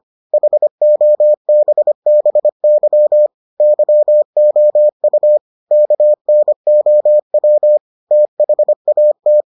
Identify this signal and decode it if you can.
Morse code, heard as a keyed tone, about 25 wpm, 600 Hz. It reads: HOBBY YOU KNOW THAT